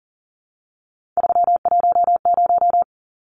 Morse code, heard as a keyed tone, about 40 wpm, 700 Hz. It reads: 310
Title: Numbers - 3-Digit 40wpm